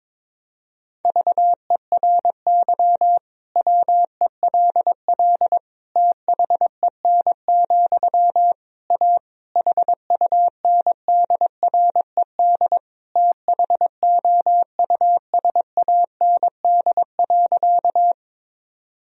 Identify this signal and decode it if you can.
Morse code, heard as a keyed tone, about 22 wpm, 700 Hz. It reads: VERY WELL THEN, A HUNDRED THOUSAND.
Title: Sentences from Top 1000 Words - Review Part 3 - 22wpm